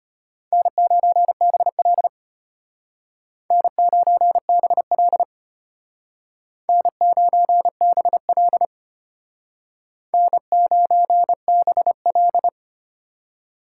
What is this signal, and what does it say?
Morse code, heard as a keyed tone, about 33 wpm, 700 Hz. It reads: N9BL N9BL N9BL N9BL